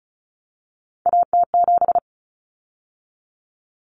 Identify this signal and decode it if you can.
Morse code, heard as a keyed tone, about 34 wpm, 700 Hz. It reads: AT7
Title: Common Cut Numbers - 3-Digit 35wpm